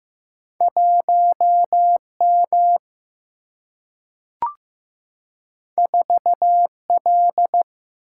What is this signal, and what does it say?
Morse code, heard as a keyed tone, about 15 wpm, 700 Hz. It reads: 1M 4L
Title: Single-Letter Single-Number 15wpm